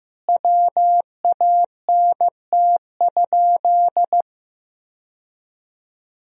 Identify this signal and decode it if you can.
Morse code, heard as a keyed tone, about 15 wpm, 700 Hz. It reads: WANT?
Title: 2x Word-Spacing - Sentences from Top 400 Words - 15wpm